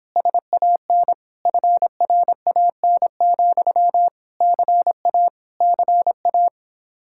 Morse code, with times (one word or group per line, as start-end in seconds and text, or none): 0.2-1.1 SAN
1.5-4.1 FRAN,
4.4-5.3 CA
5.6-6.5 CA